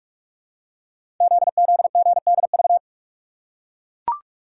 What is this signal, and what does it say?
Morse code, heard as a keyed tone, about 44 wpm, 700 Hz. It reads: ZZGDV
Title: Code Groups - 5 Letters - 45wpm